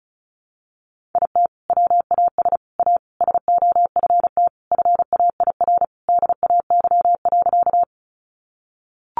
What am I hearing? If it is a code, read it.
Morse code, heard as a keyed tone, about 34 wpm, 700 Hz. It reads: IT WAS A SOFT FAIR DAY.
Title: Sentences from Top 1500 Words 35wpm